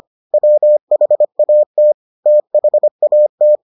Morse code, heard as a keyed tone, about 25 wpm, 600 Hz. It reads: WHAT THAT